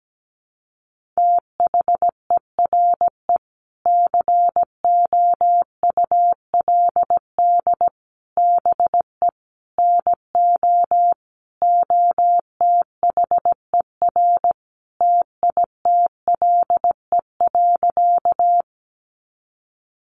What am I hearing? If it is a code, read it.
Morse code, heard as a keyed tone, about 17 wpm, 700 Hz. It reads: THERE COULD BE NO OTHER TITLE.
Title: Sentences from Top 1200 Words 17wpm